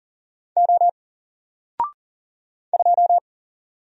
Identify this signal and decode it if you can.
Morse code, heard as a keyed tone, about 40 wpm, 700 Hz. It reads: O 2
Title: Single Letter-Number 40wpm